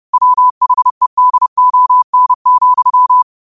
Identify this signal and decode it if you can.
Morse code, heard as a keyed tone, about 30 wpm, 1000 Hz. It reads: WHEDON,
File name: qotdfiles / 30wpm